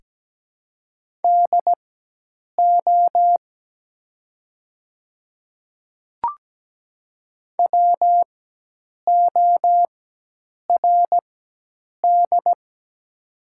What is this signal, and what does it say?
Morse code, heard as a keyed tone, about 17 wpm, spaced as for 8 wpm, 700 Hz. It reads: DO WORD
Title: 4x Character Spacing - Top 100 Words - 17wpm